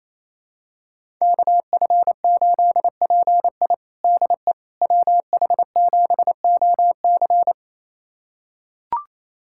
Morse code, heard as a keyed tone, about 28 wpm, 700 Hz. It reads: KF8PI DE WH7OC
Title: USA Callsign DE Callsign - 5 Characters - 28wpm